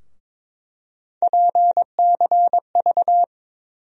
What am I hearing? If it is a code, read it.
Morse code, heard as a keyed tone, about 22 wpm, 700 Hz. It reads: PCV
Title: USA Call Sign Postfixes - 3 Letters 22wpm